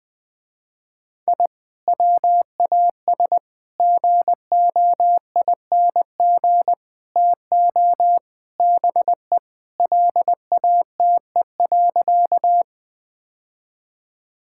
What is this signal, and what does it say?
Morse code, heard as a keyed tone, about 20 wpm, 700 Hz. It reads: I WAS GOING TO BE LATE.
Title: Sentences with 5-Word Phrases - Spoken Beginning - 20wpm